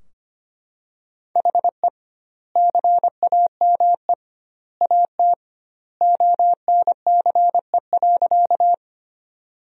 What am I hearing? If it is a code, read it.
Morse code, heard as a keyed tone, about 25 wpm, 700 Hz. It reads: HE CAME AT ONCE.
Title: 2x Word-Spacing - Sentences from Top 300 Words - 25wpm